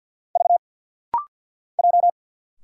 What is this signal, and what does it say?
Morse code, heard as a keyed tone, about 50 wpm, 690 Hz. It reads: V J
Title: Single Letter-Number Alt - Rapid-Fire 50wpm